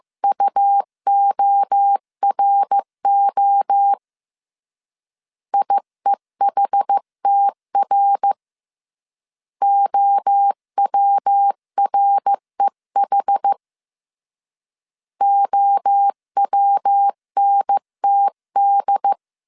Morse code, heard as a keyed tone, about 15 wpm, 800 Hz. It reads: UORO IEHTR OWREH OWNTD
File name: CW / DET / 1 3 1500C S A 14wpm 2447